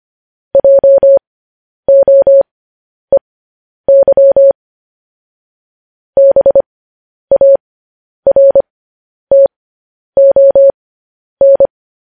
Morse code, heard as a keyed tone, about 25 wpm, spaced as for 10 wpm, 550 Hz. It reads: JOEY BARTON
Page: news-headlines-in-morse-code-at-10-wpm